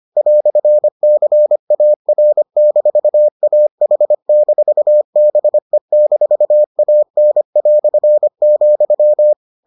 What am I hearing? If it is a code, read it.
Morse code, heard as a keyed tone, about 25 wpm, 600 Hz. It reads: "CAR-AH-BE-AN",